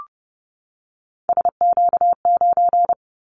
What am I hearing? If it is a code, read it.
Morse code, heard as a keyed tone, about 30 wpm, 700 Hz. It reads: SQ9
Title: Callsign Prefixes for Top 25 DXCC - 30wpm